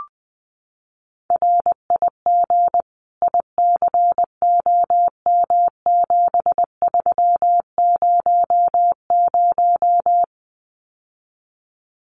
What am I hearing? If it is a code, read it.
Morse code, heard as a keyed tone, about 20 wpm, 700 Hz. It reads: RIG ICOM7300
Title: Common QSO Phrases Encore 20wpm